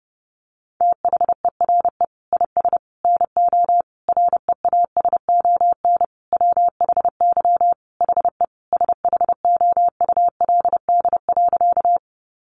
Morse code, heard as a keyed tone, about 30 wpm, 700 Hz. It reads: THERE IS NO REASON WHY HE SHOULD.
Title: Sentences from Top 700 Words 30wpm